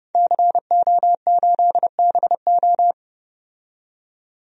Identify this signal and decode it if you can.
Morse code, heard as a keyed tone, about 32 wpm, 700 Hz. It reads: CO8BO
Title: DXCC-100 Call Signs - 5 Characters - 30wpm